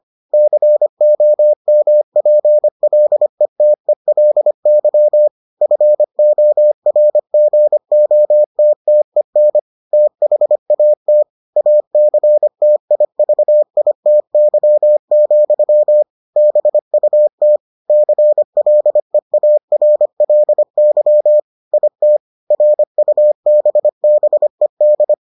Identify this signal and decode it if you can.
Morse code, heard as a keyed tone, about 25 wpm, 600 Hz. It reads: COMPLETELY FORGOTTEN THAT ACTIVITY, BUT CLEARLY IT RUBBED